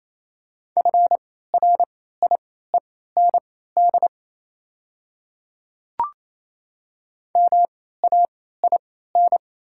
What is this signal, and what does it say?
Morse code, heard as a keyed tone, about 28 wpm, spaced as for 16 wpm, 700 Hz. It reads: FRIEND MAIN
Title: 3x Character Spacing - Top 300 Words - 28wpm